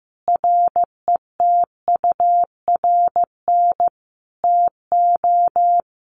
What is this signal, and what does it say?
Morse code, heard as a keyed tone, about 15 wpm, 700 Hz. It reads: RETURN TO
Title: Sentences from Top 800 Words 15wpm